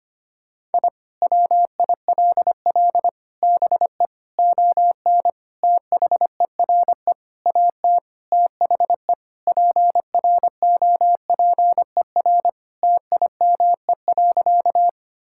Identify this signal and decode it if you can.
Morse code, heard as a keyed tone, about 25 wpm, 700 Hz. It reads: I WILL BE ON THERE AT THE PROPER TIME.
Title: Sentences from Top 1000 Words - Review Part 5 - 25wpm